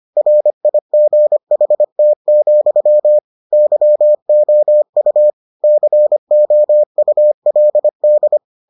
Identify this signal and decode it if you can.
Morse code, heard as a keyed tone, about 25 wpm, 600 Hz. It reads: RIGHT, YOU COULD